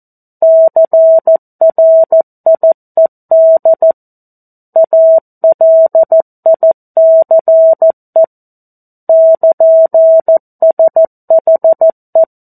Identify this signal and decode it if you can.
Morse code, heard as a keyed tone, about 14 wpm, 650 Hz. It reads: CRIED ALICE <KN>SHE
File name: 02-Chapter-02 14 WPM